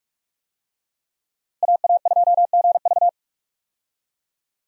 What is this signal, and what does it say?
Morse code, heard as a keyed tone, about 47 wpm, 700 Hz. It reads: AA2GV
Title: USA Call Signs - 5 Characters - Repeated 2x - 45wpm